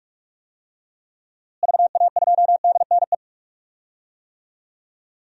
Morse code, heard as a keyed tone, about 42 wpm, 700 Hz. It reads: VA2DNE
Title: DXCC-100 Call Signs - 6 Characters - Repeated 2x - 45wpm